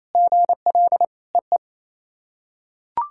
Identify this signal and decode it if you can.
Morse code, heard as a keyed tone, about 28 wpm, 700 Hz. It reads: GL EE
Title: SOTA QSO Practice 28wpm